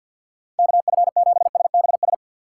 Morse code, heard as a keyed tone, about 61 wpm, 700 Hz. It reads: KF7SBS
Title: Speed Racing - USA 2x3 Call Signs 50wpm